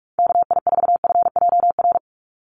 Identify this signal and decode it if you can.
Morse code, heard as a keyed tone, about 44 wpm, 700 Hz. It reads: KI4FJR
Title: USA Call Signs 45wpm